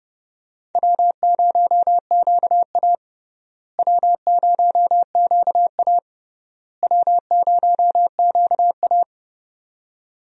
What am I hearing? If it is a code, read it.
Morse code, heard as a keyed tone, about 30 wpm, 700 Hz. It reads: W0QA W0QA W0QA
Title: USA Call Signs - 4 Characters - Repeated 3x - 30wpm